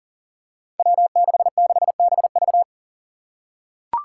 Morse code, heard as a keyed tone, about 40 wpm, 700 Hz. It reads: W66BV
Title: Five Challenging Letter-Number Combinations 40wpm